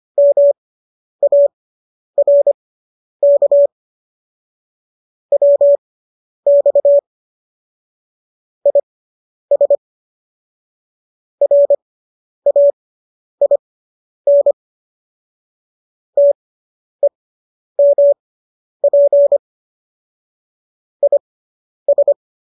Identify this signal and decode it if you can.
Morse code, heard as a keyed tone, about 25 wpm, spaced as for 10 wpm, 570 Hz. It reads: MARK WX IS RAIN TEMP IS